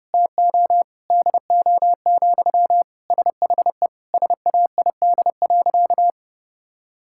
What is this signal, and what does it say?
Morse code, heard as a keyed tone, about 30 wpm, 700 Hz. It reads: TO DO, SHE SAID.
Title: Sentences from Top 1000 Words - Review Part 5 - 30wpm